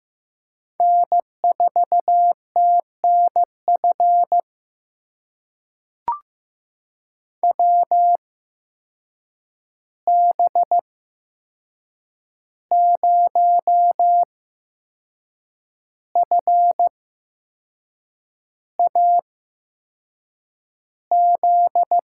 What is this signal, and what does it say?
Morse code, heard as a keyed tone, about 15 wpm, spaced as for 6 wpm, 700 Hz. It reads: N4TNF WB0FAZ
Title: USA Call Signs - 8x Character Spacing - 15wpm